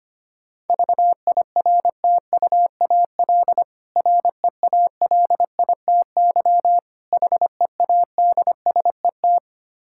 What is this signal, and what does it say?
Morse code, heard as a keyed tone, about 25 wpm, 700 Hz. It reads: VIRTUAL REALITY HEADSET
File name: practice / 25wpm / Difficult Words with Hints - 25wpm